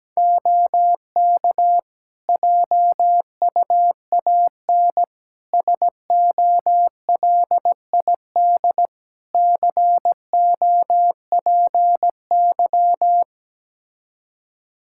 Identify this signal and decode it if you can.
Morse code, heard as a keyed tone, about 17 wpm, 700 Hz. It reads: OK JUAN SOLID COPY
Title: Practice QSO - Exchange 3 - More Info - 17wpm